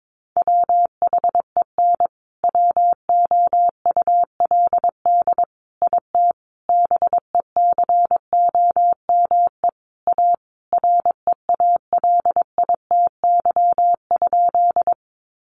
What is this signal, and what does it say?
Morse code, heard as a keyed tone, about 22 wpm, 700 Hz. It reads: WHEN WOULD IT BECOME A REALITY?
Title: Sentences from Top 1200 Words 22wpm